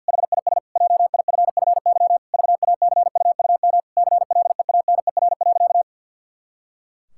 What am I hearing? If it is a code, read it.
Morse code, heard as a keyed tone, about 52 wpm, 700 Hz. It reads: HIS JIFFY VACUUM CLEANER.
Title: Pangram Challenge v2 - 50wpm